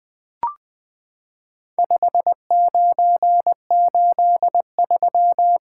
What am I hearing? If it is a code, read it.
Morse code, heard as a keyed tone, about 20 wpm, 700 Hz. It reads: 5983
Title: Four Letter-Number Combinations - 20wpm